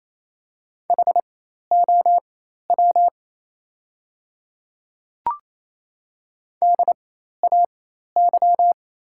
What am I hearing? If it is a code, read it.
Morse code, heard as a keyed tone, about 28 wpm, spaced as for 13 wpm, 700 Hz. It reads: HOW DAY
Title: 4x Character Spacing - Top 100 Words - 28wpm